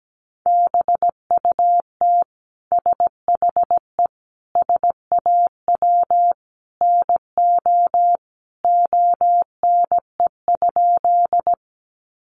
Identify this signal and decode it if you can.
Morse code, heard as a keyed tone, about 17 wpm, 700 Hz. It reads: BUT SHE SAW NO ONE?
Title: Sentences from Top 300 Words 17wpm